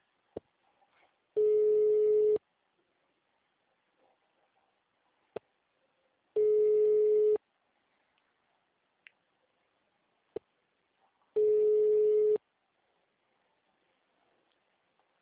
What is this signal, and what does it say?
Ringback tone (Europe, 425 Hz).
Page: Retencao Port